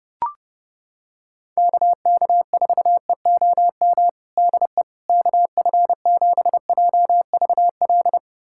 Morse code, heard as a keyed tone, about 29 wpm, 700 Hz. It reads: KK4EOM DE KF7JVL
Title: USA Callsign DE Callsign - 6 Characters - 30wpm